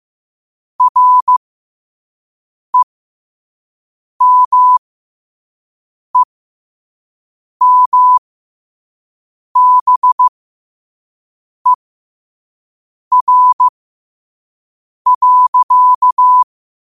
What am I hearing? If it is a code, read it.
Morse code, heard as a keyed tone, about 15 wpm, spaced as for 5 wpm, 1000 Hz. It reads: REMEMBER.
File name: qotdfiles / 5wpm